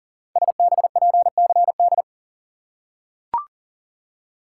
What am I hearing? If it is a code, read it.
Morse code, heard as a keyed tone, about 39 wpm, 700 Hz. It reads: SBPCD
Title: Code Groups - 5 Letters - 40wpm